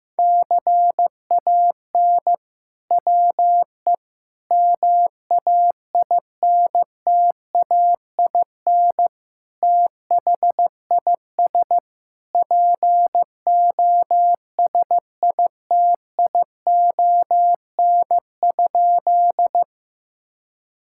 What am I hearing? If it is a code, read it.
Morse code, heard as a keyed tone, about 15 wpm, 700 Hz. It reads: CAN WE MAINTAIN THIS POSITION?
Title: Sentences from Top 1000 Words - Review Part 5 - 15wpm